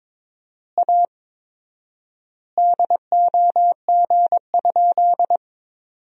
Morse code, heard as a keyed tone, about 22 wpm, 700 Hz. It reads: A DOG?
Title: 4x Word-Spacing - Sentences from Top 400 Words - 22wpm